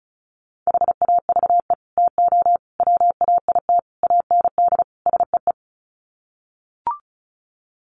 Morse code, heard as a keyed tone, about 34 wpm, 700 Hz. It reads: HAVE TO WAIT AND SEE
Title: Sets of 5 Words Encore2 35wpm